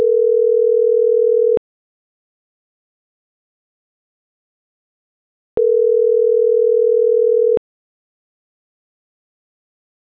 Ringback tone (North America), 440 + 480 Hz.